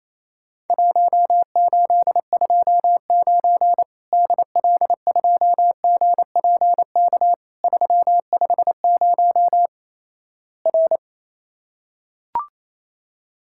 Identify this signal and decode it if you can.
Morse code, heard as a keyed tone, about 28 wpm, 700 Hz. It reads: 1829 DL2GPK 350 R
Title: Worked All Europe DX Contest - QTC Exchange - 28wpm